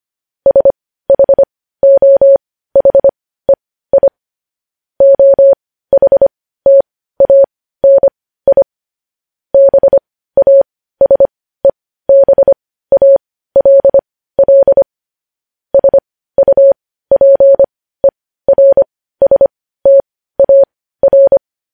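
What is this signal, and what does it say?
Morse code, heard as a keyed tone, about 25 wpm, spaced as for 15 wpm, 550 Hz. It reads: SHOHEI OHTANI BASEBALL SUPERSTAR
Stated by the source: Morse code news headlines at 15 wpm.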